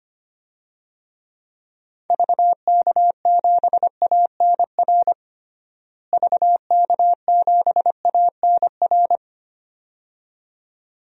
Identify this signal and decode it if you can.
Morse code, heard as a keyed tone, about 25 wpm, 700 Hz. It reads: VK7ANR VK7ANR
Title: DXCC-100 Call Signs - 6 Characters - Repeated 2x - 25wpm